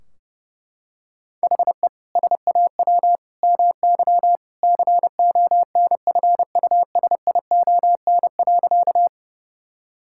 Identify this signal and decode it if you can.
Morse code, heard as a keyed tone, about 30 wpm, 700 Hz. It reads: HE SAW MY CONFUSION.
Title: Sentences from Top 1600 Words 30wpm